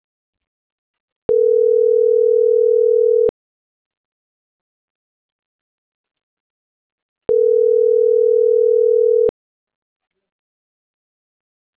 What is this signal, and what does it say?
Ringback tone (North America), 440 + 480 Hz.